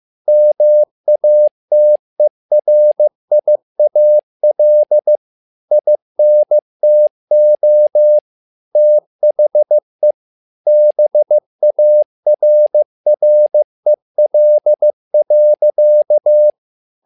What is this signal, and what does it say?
Morse code, heard as a keyed tone, about 15 wpm, 600 Hz. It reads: MATERIAL INTO THE BARREL.
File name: learning center / tower trouble / tower trouble 15wpm